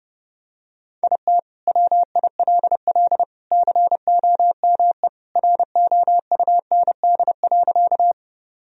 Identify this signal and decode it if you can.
Morse code, heard as a keyed tone, about 32 wpm, 700 Hz. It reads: IT WILL COME ROUND.
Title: Sentences from Top 1000 Words - Review Part 1 - 30wpm